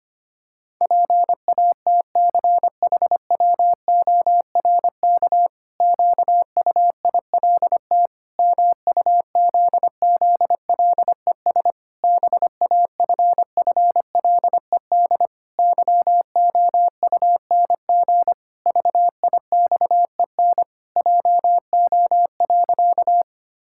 Morse code, heard as a keyed tone, about 25 wpm, 700 Hz. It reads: PATCHWORK QUILT MUZZLES BAFFLED YOUNG VIXEN JO.
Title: Pangram Challenge v2 - 25wpm